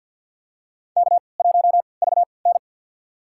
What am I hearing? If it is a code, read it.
Morse code, heard as a keyed tone, about 53 wpm, spaced as for 29 wpm, 700 Hz. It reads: K1VN
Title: USA Call Signs - 3x Character Spacing - 50wpm